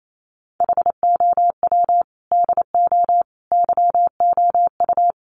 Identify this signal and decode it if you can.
Morse code, heard as a keyed tone, about 28 wpm, 700 Hz. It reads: HOW DO YOU
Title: Sets of 5 Words Encore 28wpm